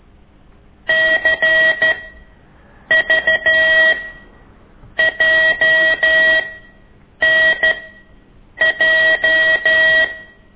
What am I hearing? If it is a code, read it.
Morse code, heard as a keyed tone, about 13 wpm, spaced as for 7 wpm, 1900 Hz. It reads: CVJNJ
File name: ik1weg2 / CORSI CW